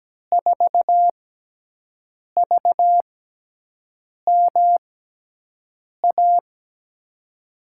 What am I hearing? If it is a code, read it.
Morse code, heard as a keyed tone, about 17 wpm, spaced as for 6 wpm, 700 Hz. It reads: 4VMA
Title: USA Call Signs - 6 Characters - 6x Character Spacing - 17wpm